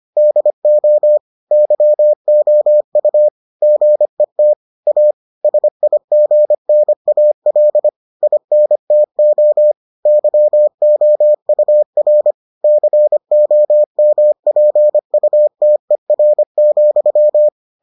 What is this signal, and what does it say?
Morse code, heard as a keyed tone, about 25 wpm, 600 Hz. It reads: DO YOU GET A SIGNAL INTO YOUR COMPUTER,